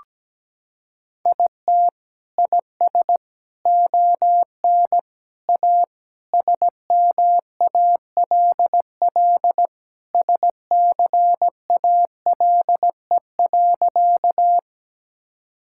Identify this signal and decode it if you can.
Morse code, heard as a keyed tone, about 17 wpm, 700 Hz. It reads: IT IS ON A SMALL SCALE.
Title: Sentences from Top 1000 Words - Review Part 5 - 17wpm